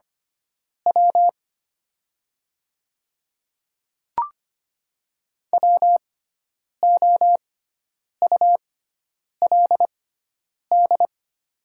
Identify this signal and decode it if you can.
Morse code, heard as a keyed tone, about 25 wpm, spaced as for 9 wpm, 700 Hz. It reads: W WOULD
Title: 6x Character Spacing - Top 100 Words - 25wpm